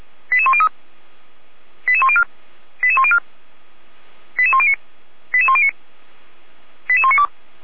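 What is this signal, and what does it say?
Five-tone selective call (ZVEI), heard as quick runs of five tones, tones of about 70 ms. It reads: 88103 88104 88104 88109 88109 88192
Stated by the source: ZVEI five-tone selective call 88105 received on SDR, from the feuerwehr-orlamuende.org fire brigade audio archive.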